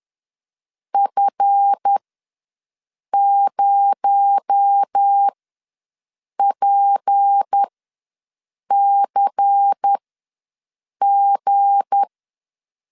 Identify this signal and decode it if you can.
Morse code, heard as a keyed tone, about 10 wpm, spaced as for 6 wpm, 800 Hz. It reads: F0PCG
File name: LEZIONI / 7 GF9Z / 7 GF ws 10wpm 3918